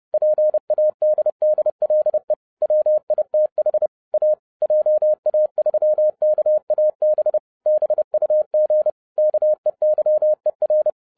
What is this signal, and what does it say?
Morse code, heard as a keyed tone, about 30 wpm, 610 Hz. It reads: PADDLE WITH A JA3KAB BUG KEYER
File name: 2011-07-mp3 / n20 GREETINGS FROM SINGAPORE 30wpm